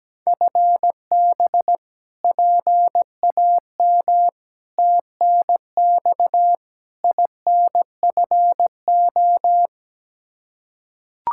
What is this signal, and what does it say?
Morse code, heard as a keyed tone, about 17 wpm, 700 Hz. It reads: FB PAM TNX INFO
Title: Practice QSO - Exchange 3 - More Info - 17wpm